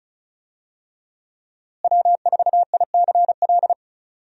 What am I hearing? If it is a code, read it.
Morse code, heard as a keyed tone, about 36 wpm, 700 Hz. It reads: W4ICL
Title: USA Call Signs - 5 Characters - Repeated 3x - 35wpm